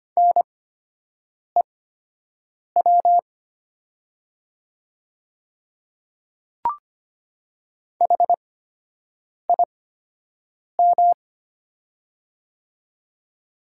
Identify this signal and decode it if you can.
Morse code, heard as a keyed tone, about 25 wpm, spaced as for 7 wpm, 700 Hz. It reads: NEW HIM
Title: 8x Character Spacing - Top 300 Words - 25wpm